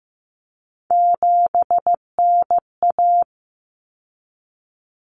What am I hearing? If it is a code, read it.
Morse code, heard as a keyed tone, about 15 wpm, 700 Hz. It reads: 7NA
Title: Common Cut Numbers - 3-Digit 15wpm